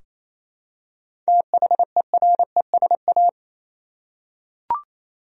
Morse code, heard as a keyed tone, about 28 wpm, 700 Hz. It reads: THERESA
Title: USA Common Names - 28wpm